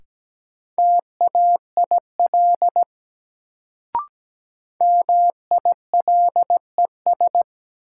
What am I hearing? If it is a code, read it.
Morse code, heard as a keyed tone, about 17 wpm, 700 Hz. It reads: TAIL MILES
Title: Sentences from Top 1000 Words - Review Part 3 - New Word Drill - 17wpm